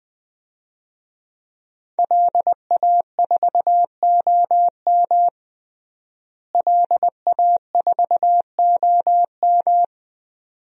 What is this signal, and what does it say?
Morse code, heard as a keyed tone, about 20 wpm, 700 Hz. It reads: LA4OM LA4OM
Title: DXCC-100 Call Signs - 5 Characters - Repeated 2x - 20wpm